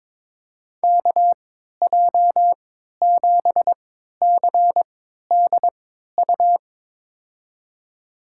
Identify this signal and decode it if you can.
Morse code, heard as a keyed tone, about 22 wpm, spaced as for 12 wpm, 700 Hz. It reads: KJ7CDU
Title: USA Call Signs - 6 Characters - 3x Character Spacing - 22wpm